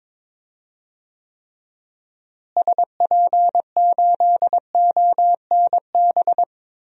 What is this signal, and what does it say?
Morse code, heard as a keyed tone, about 22 wpm, 700 Hz. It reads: SP8ONB
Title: DXCC-100 Call Signs - 6 Characters - Repeated 3x - 22wpm